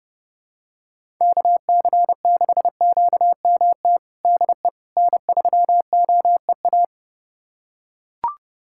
Morse code, heard as a keyed tone, about 32 wpm, 700 Hz. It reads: KC6QMT DE N3OEA
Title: USA Callsign DE Callsign 30wpm